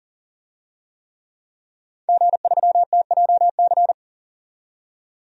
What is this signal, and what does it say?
Morse code, heard as a keyed tone, about 37 wpm, 700 Hz. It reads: G3TJC
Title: DXCC-100 Call Signs - 5 Characters - 40wpm